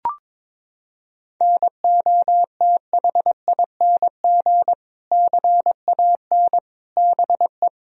Morse code, heard as a keyed tone, about 22 wpm, 700 Hz. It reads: NOTHING CAN BE
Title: Sentences from Top 1000 Words - Review Part 2 - 22wpm